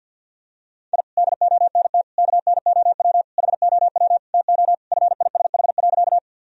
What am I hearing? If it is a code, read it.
Morse code, heard as a keyed tone, about 50 wpm, 700 Hz. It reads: I DONT KNOW HOW TO FISH.
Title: Sentences from Top 1000 Words - Review Part 1 - 50wpm